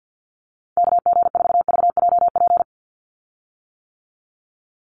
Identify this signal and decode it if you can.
Morse code, heard as a keyed tone, about 50 wpm, 700 Hz. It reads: KG4VJP
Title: USA Call Signs 50wpm